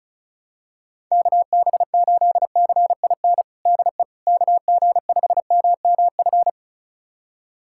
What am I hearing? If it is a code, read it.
Morse code, heard as a keyed tone, about 36 wpm, 700 Hz. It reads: KB8CIN DE KG5MMF
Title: USA Callsign DE Callsign - 6 Characters - 35wpm